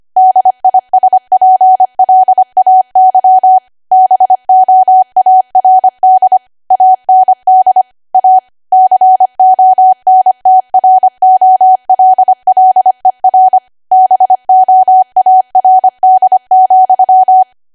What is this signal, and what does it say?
Morse code, heard as a keyed tone, about 25 wpm, 750 Hz. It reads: DISPLAY BOARD AND A CONTROLLER BOARD,
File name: cwa / W1AW-Level3 / 25WPM